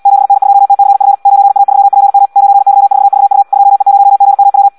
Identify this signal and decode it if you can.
Morse code, heard as a keyed tone, about 78 wpm, 800 Hz. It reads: 7N2TCF 9A3PM 9Y4VU 2E0GMW
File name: Audio practica CW / qrq-tx-78 2